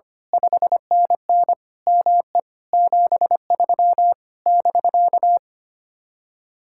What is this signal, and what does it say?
Morse code, heard as a keyed tone, about 24 wpm, 700 Hz. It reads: 5NN ME 73 <BK>
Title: Speed-Racing Common QSO Phrases - 25wpm